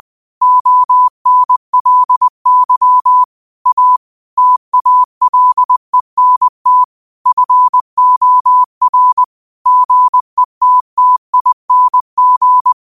Morse code, heard as a keyed tone, about 20 wpm, 1000 Hz. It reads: ONLY A TALENT FOR GETTING
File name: qotdfiles / 20wpm